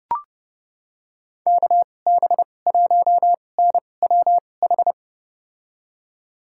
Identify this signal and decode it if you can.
Morse code, heard as a keyed tone, about 30 wpm, spaced as for 22 wpm, 700 Hz. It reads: KB1NWH
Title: USA Call Signs - 6 Characters - 2x Character Spacing - 30wpm